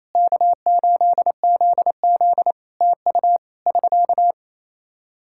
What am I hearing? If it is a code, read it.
Morse code, heard as a keyed tone, about 28 wpm, 700 Hz. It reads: K8ZZ TU <SK>
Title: Practice QSO - Exchange 4 - Ending - 28wpm